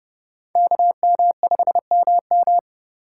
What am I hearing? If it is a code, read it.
Morse code, heard as a keyed tone, about 36 wpm, 700 Hz. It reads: KM5MM
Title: Speed Racing - USA Call Signs 30wpm